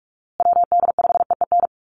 Morse code, heard as a keyed tone, about 44 wpm, 700 Hz. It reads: WD5EEN
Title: USA Call Signs Repeated 3x 45wpm